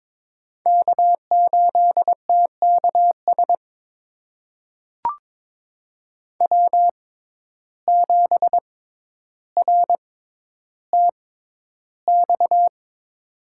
Morse code, heard as a keyed tone, about 22 wpm, spaced as for 11 wpm, 700 Hz. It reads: K8TKS W7RTX
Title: USA Call Signs - 6x Character Spacing - 22wpm